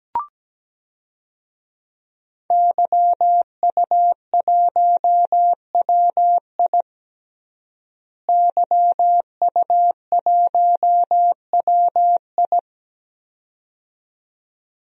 Morse code, heard as a keyed tone, about 17 wpm, 700 Hz. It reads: YU1WI YU1WI
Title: DXCC-100 Call Signs - 5 Characters - Repeated 2x - 17wpm